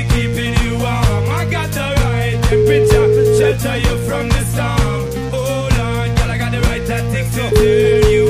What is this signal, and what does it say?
Ringback tone (Europe, 425 Hz), with noise.